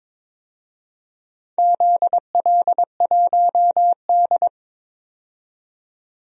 Morse code, heard as a keyed tone, about 22 wpm, 700 Hz. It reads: ZL1D
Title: DXCC-100 Call Signs - 4 Characters - 22wpm